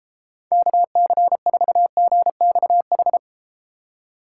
Morse code, heard as a keyed tone, about 31 wpm, 700 Hz. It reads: KC4GXH